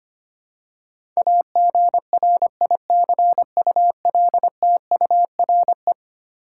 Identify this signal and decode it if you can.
Morse code, heard as a keyed tone, about 25 wpm, 700 Hz. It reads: AGRICULTURE
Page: morse-code-ninja-practice-25wpm